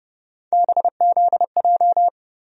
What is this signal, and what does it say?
Morse code, heard as a keyed tone, about 29 wpm, 700 Hz. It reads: BZJ
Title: USA Call Sign Postfixes - 3 Letters 30wpm